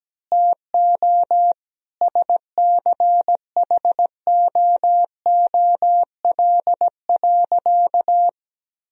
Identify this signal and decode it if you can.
Morse code, heard as a keyed tone, about 17 wpm, 700 Hz. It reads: TO SCHOOL.
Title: Sentences from Top 1000 Words - Review Part 3 - 17wpm